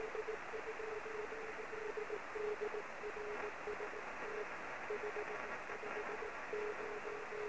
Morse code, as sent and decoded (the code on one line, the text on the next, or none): none